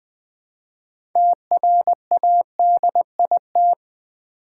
Morse code, heard as a keyed tone, about 20 wpm, 700 Hz. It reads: TRADIT
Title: Build A Word - 3 Letters At A Time - 9-Letter Words - 20wpm